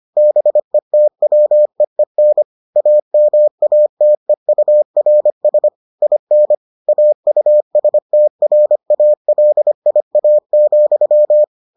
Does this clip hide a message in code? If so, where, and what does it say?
Yes: Morse code from 0.2 to 11.4 s, heard as a keyed tone, about 25 wpm, 600 Hz. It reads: BETWEEN AMATEURS IN AUSTRALIA,